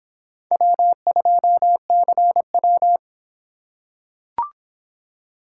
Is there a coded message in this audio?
Yes: Morse code from 0.5 to 3.0 s, heard as a keyed tone, about 26 wpm, 700 Hz. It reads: W2CW